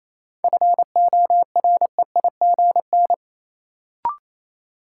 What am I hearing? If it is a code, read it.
Morse code, heard as a keyed tone, about 28 wpm, 700 Hz. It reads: FOREIGN
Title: Sentences from Top 1400 Words - New Word Drill Alt - 28wpm